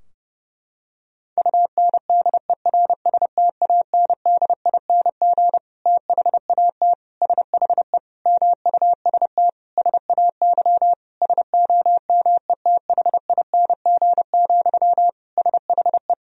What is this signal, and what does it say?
Morse code, heard as a keyed tone, about 30 wpm, 700 Hz. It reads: UNDERSTANDING THAT SHE MUST SAY SOMETHING, SHE
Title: Sentences from Top 1000 Words - Review Part 3 - 30wpm